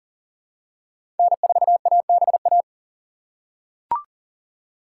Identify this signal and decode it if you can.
Morse code, heard as a keyed tone, about 37 wpm, 700 Hz. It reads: N4ABA
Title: USA Call Signs - 8x Character Spacing - 40wpm